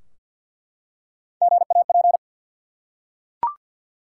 Morse code, heard as a keyed tone, about 48 wpm, 700 Hz. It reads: GAP
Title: Three Letter Words - 50wpm